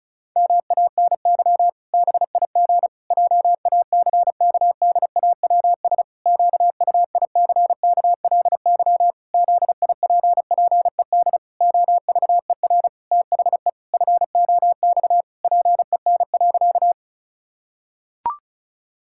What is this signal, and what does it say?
Morse code, heard as a keyed tone, about 34 wpm, 700 Hz. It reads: MANY BIG JACKDAWS QUICKLY ZIPPED OVER THE FOX PEN.
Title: Pangram Challenge v2 - 35wpm